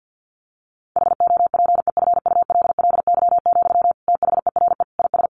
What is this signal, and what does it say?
Morse code, heard as a keyed tone, about 50 wpm, 700 Hz. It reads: HOPEFULLY, THERE IS
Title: Sentences from Top 1500 Words 50wpm